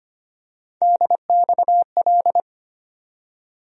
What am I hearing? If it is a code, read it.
Morse code, heard as a keyed tone, about 25 wpm, 700 Hz. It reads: DXL
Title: USA Call Sign Postfixes - 3 Letters 25wpm